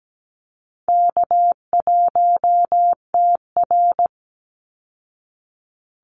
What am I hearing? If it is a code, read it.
Morse code, heard as a keyed tone, about 17 wpm, 700 Hz. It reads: K1TR
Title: Easy Call Signs Repeated 3x - 17wpm